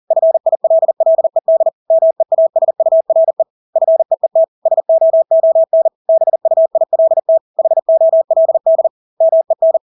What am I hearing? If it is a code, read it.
Morse code, heard as a keyed tone, about 40 wpm, 650 Hz. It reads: FILLED MEASURE FEET SOON BUILT HOLD MEN
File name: Engelske ord 40-40 wpm